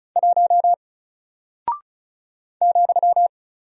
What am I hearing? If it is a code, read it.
Morse code, heard as a keyed tone, about 36 wpm, 700 Hz. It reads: 1 ,
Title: Single Character (Letter:Number:Punctuation:Prosign) Alt 35wpm